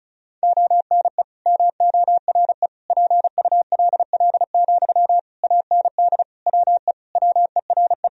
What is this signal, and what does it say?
Morse code, heard as a keyed tone, about 36 wpm, 700 Hz. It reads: ONE MORE PULL, AND WE WERE
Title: Sentences from Top 1000 Words - Review Part 2 - 35wpm